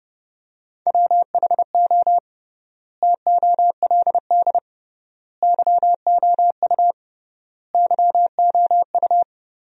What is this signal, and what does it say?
Morse code, heard as a keyed tone, about 30 wpm, 700 Hz. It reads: WHO TOLD YOU YOU
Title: 3x Word-Spacing - Sentences from Top 400 Words - 30wpm